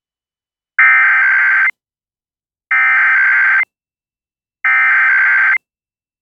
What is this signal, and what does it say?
Emergency Alert System (SAME) tones (header bursts), heard as harsh buzzing bursts. It is heard from 0.8 to 5.5 s.